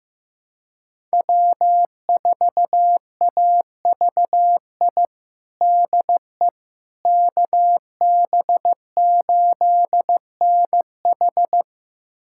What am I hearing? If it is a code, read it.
Morse code, heard as a keyed tone, about 16 wpm, 700 Hz. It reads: W4AVI DE KB8NH
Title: USA Callsign DE Callsign - 5 Characters - 15wpm